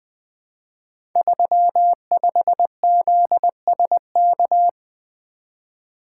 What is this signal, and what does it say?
Morse code, heard as a keyed tone, about 20 wpm, 700 Hz. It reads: 35ZSK
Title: Code Groups - 5 Letter:Numbers - 20wpm